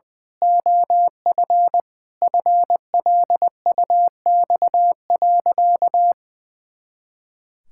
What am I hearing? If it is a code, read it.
Morse code, heard as a keyed tone, about 21 wpm, 700 Hz. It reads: OF FLUX.
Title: Pangram Challenge v2 - 20wpm